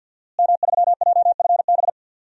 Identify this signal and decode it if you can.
Morse code, heard as a keyed tone, about 53 wpm, 700 Hz. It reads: M3JFB
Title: DXCC-100 Call Signs - 5 Characters - Repeated 2x - 50wpm